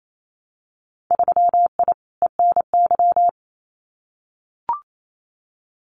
Morse code, heard as a keyed tone, about 28 wpm, 700 Hz. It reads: 3I ENY
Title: Winter Field Day Exchanges Encore 28wpm